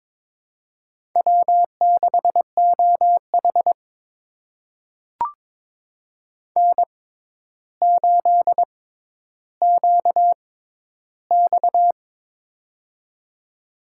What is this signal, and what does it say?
Morse code, heard as a keyed tone, about 22 wpm, spaced as for 11 wpm, 700 Hz. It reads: W6OH N8QX
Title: USA Call Signs - 4 Characters - 6x Character Spacing - 22wpm